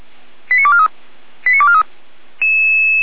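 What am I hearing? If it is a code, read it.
Five-tone selective call (ZVEI), heard as quick runs of five tones, tones of about 70 ms. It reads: 89243 89243
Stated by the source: Five-tone ZVEI selective call received by SDR, from the audio server of Feuerwehr Orlamünde.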